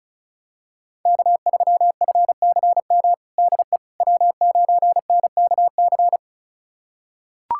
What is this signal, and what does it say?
Morse code, heard as a keyed tone, about 36 wpm, 700 Hz. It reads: K3FCM DE W9NKC
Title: USA Callsign DE Callsign - 5 Characters - 35wpm